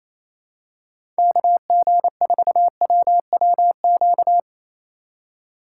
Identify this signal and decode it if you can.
Morse code, heard as a keyed tone, about 28 wpm, 700 Hz. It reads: KG4WWQ
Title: USA Call Signs - 6 Characters - 2x Character Spacing - 28wpm